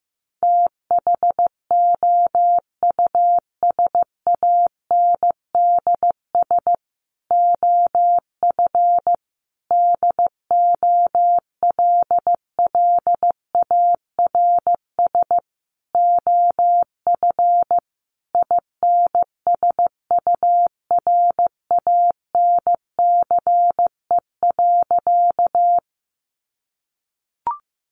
Morse code, heard as a keyed tone, about 15 wpm, 700 Hz. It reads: THOUSANDS OF DOLLARS OF INSURANCE.
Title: A Sound Of Thunder 15wpm